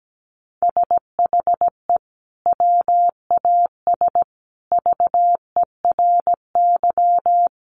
Morse code, heard as a keyed tone, about 17 wpm, 700 Hz. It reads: SHE WAS VERY
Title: Sentences from Top 1400 Words 17wpm